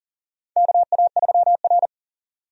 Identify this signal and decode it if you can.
Morse code, heard as a keyed tone, about 40 wpm, 700 Hz. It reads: KA3R
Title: USA Call Signs - 4 Characters - 40wpm